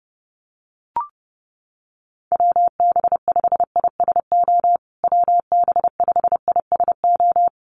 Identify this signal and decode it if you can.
Morse code, heard as a keyed tone, about 28 wpm, 700 Hz. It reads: WB5ISO WB5ISO
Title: USA Call Signs Repeated 3x 30wpm